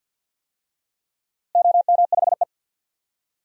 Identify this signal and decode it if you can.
Morse code, heard as a keyed tone, about 51 wpm, 700 Hz. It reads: OM5E
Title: DXCC-100 Call Signs - 4 Characters - Repeated 2x - 50wpm